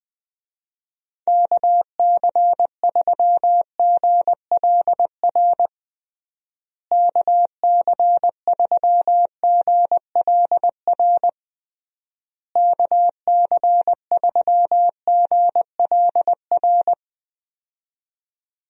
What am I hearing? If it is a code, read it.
Morse code, heard as a keyed tone, about 20 wpm, 700 Hz. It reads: KC3GLR KC3GLR KC3GLR
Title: USA Call Signs - 6 Characters - Repeated 3x - 20wpm